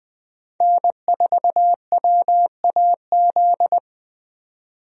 Morse code, heard as a keyed tone, about 20 wpm, 700 Hz. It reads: N4WAZ
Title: USA Call Signs - 5 Characters - 20wpm